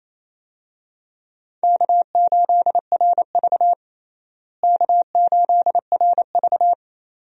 Morse code, heard as a keyed tone, about 28 wpm, 700 Hz. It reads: K8RV K8RV
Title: USA Call Signs - 4 Characters - Repeated 2x - 28wpm